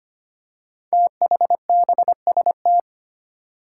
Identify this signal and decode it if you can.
Morse code, heard as a keyed tone, about 25 wpm, 700 Hz. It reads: THBST
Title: Code Groups - 5 Letters - 25wpm